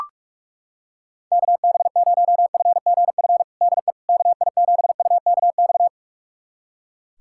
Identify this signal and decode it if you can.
Morse code, heard as a keyed tone, about 43 wpm, 700 Hz. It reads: KB0FGF DE KI7UKX
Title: USA Callsign DE Callsign - 6 Characters - 45wpm